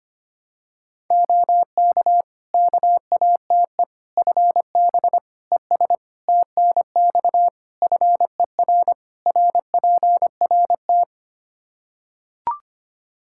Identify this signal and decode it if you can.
Morse code, heard as a keyed tone, about 25 wpm, 700 Hz. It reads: OK KATE FB ES TNX FER RPRT
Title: Practice QSO - Exchange 2 - Info - 25wpm